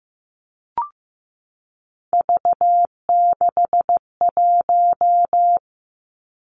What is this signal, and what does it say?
Morse code, heard as a keyed tone, about 15 wpm, 700 Hz. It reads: V61
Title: Three Challenging Letter-Number Combinations 15wpm